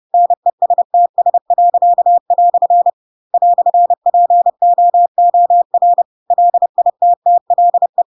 Morse code, heard as a keyed tone, about 30 wpm, 700 Hz. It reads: NESTS." "POOR LITTLE
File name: Pinocchio / Pinocchio ch4 30 30 ik1hge